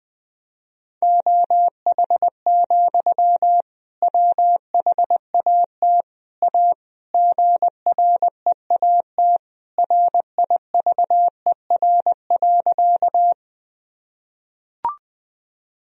Morse code, heard as a keyed tone, about 19 wpm, 700 Hz. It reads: OH, WHAT A GREAT RIVER.
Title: Sentences from Top 1000 Words - Review Part 1 - 20wpm